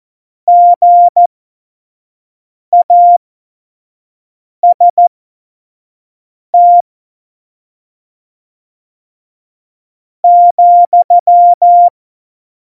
Morse code, heard as a keyed tone, about 14 wpm, spaced as for 4 wpm, 700 Hz. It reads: GAST,
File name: Morsekurs / Witze / Witz 12 Mai 2020, 25 F70